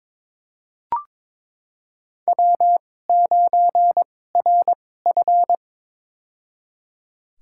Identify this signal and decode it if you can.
Morse code, heard as a keyed tone, about 21 wpm, spaced as for 16 wpm, 700 Hz. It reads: W9RF
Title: USA Call Signs - 4 Characters - 2x Character Spacing - 22wpm